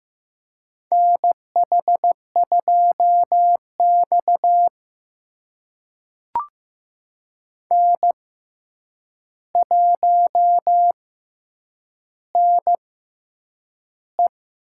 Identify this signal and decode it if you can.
Morse code, heard as a keyed tone, about 15 wpm, spaced as for 8 wpm, 700 Hz. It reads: NH2X N1NE
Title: USA Call Signs - 4 Characters - 6x Character Spacing - 15wpm